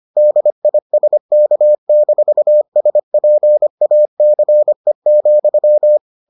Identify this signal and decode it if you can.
Morse code, heard as a keyed tone, about 25 wpm, 600 Hz. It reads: DISK-SPACE,